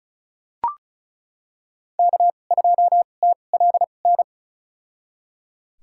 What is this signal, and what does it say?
Morse code, heard as a keyed tone, about 36 wpm, spaced as for 25 wpm, 700 Hz. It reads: K2TLN
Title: USA Call Signs - 5 Characters - 2x Character Spacing - 35wpm